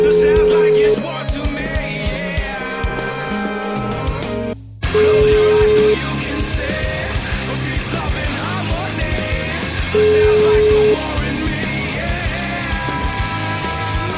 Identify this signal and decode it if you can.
Ringback tone (Europe, 425 Hz), with speech.